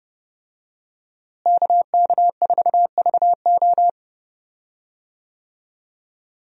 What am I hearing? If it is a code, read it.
Morse code, heard as a keyed tone, about 30 wpm, 700 Hz. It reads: KK4VO
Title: USA Call Signs - 5 Characters - Repeated 2x - 30wpm